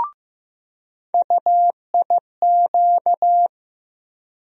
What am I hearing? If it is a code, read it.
Morse code, heard as a keyed tone, about 15 wpm, 700 Hz. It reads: UIQ
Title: USA Call Sign Postfixes - 3 Letters 15wpm